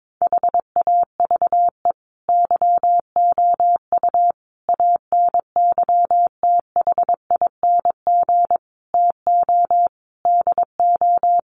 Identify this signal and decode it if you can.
Morse code, heard as a keyed tone, about 22 wpm, 700 Hz. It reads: HAVE YOU ANYTHING TO DO
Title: Sentences from Top 1000 Words 22wpm